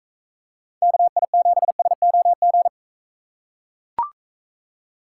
Morse code, heard as a keyed tone, about 44 wpm, 700 Hz. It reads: KI7SOG